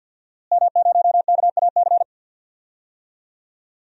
Morse code, heard as a keyed tone, about 46 wpm, 700 Hz. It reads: M0KAC
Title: DXCC-100 Call Signs - 5 Characters - Repeated 3x - 50wpm